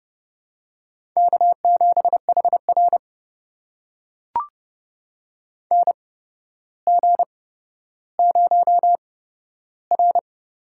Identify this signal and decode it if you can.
Morse code, heard as a keyed tone, about 30 wpm, spaced as for 13 wpm, 700 Hz. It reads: K7HR NG0R
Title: USA Call Signs - 8x Character Spacing - 30wpm